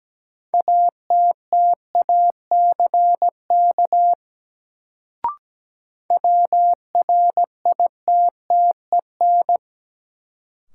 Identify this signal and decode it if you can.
Morse code, heard as a keyed tone, about 17 wpm, 700 Hz. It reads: ATTACK WRITTEN
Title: Sentences from Top 1500 Words - New Word Drill - 17wpm